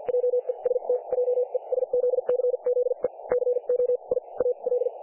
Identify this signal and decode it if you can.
Morse code, heard as a keyed tone, about 50 wpm, spaced as for 41 wpm, 500 Hz. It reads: JEST JESZCZE CO I A L
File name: dlazaawansowanych / nagraniazpasma / websdr recording 3564.7kHz